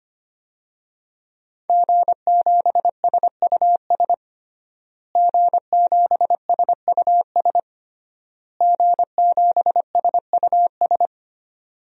Morse code, heard as a keyed tone, about 25 wpm, 700 Hz. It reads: G7SUS G7SUS G7SUS
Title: DXCC-100 Call Signs - 5 Characters - Repeated 3x - 25wpm